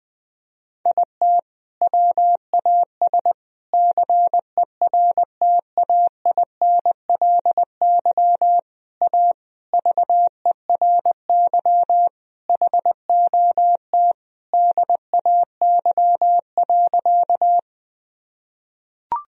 Morse code, heard as a keyed tone, about 21 wpm, 700 Hz. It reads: IT WAS CERTAINLY A VERY HOT DAY.
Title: Sentences from Top 1000 Words - Review Part 3 - 20wpm